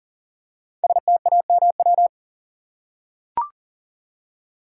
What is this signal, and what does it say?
Morse code, heard as a keyed tone, about 40 wpm, 700 Hz. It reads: STAMW
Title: Code Groups - 5 Letters - 40wpm